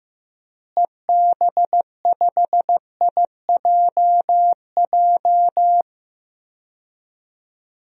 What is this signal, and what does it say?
Morse code, heard as a keyed tone, about 15 wpm, 700 Hz. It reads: EB5IJJ
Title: DXCC-100 Call Signs - 6 Characters - Repeated 2x - 15wpm